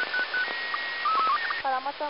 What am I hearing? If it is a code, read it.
SSTV (slow-scan television), Robot 36, with speech.